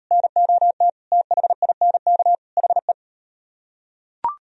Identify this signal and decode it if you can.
Morse code, heard as a keyed tone, about 34 wpm, 700 Hz. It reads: NOT THINK HE